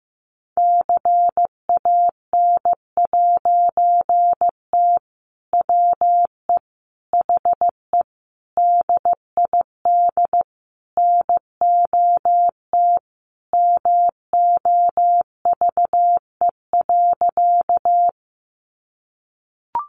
Morse code, heard as a keyed tone, about 15 wpm, 700 Hz. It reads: CAN'T WE HE DID NOT MOVE.
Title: A Sound Of Thunder 15wpm